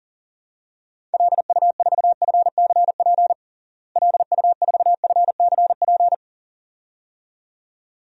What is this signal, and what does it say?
Morse code, heard as a keyed tone, about 40 wpm, 700 Hz. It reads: LU4FCP LU4FCP
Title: DXCC-100 Call Signs - 6 Characters - Repeated 2x - 40wpm